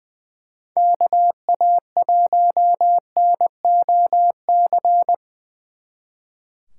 Morse code, heard as a keyed tone, about 25 wpm, 700 Hz. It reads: KA1NOC
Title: Speed Racing - USA 2x3 Call Signs 20wpm